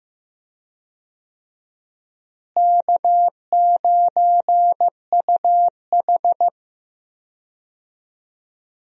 Morse code, heard as a keyed tone, about 15 wpm, 700 Hz. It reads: K9UH
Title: USA Call Signs - 4 Characters - Repeated 2x - 15wpm